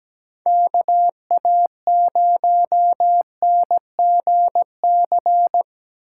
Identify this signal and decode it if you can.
Morse code, heard as a keyed tone, about 19 wpm, 700 Hz. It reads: KA0NGC
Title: Speed Racing - USA Call Signs 17wpm